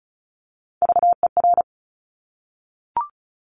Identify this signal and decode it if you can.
Morse code, heard as a keyed tone, about 36 wpm, 700 Hz. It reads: VER
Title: Top Three Letter Combinations in Words 35wpm